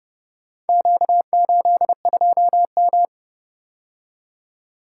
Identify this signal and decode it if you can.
Morse code, heard as a keyed tone, about 29 wpm, 700 Hz. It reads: Q82M